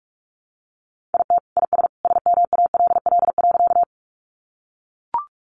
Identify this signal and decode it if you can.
Morse code, heard as a keyed tone, about 44 wpm, 700 Hz. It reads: IT IS SMALL.
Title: Additional Easy Phrases 45wpm